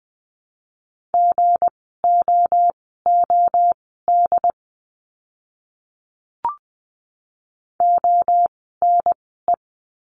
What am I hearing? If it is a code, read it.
Morse code, heard as a keyed tone, about 20 wpm, spaced as for 15 wpm, 700 Hz. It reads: GOOD ONE
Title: 244 Lesson 39f - Letter z - Words 20wpm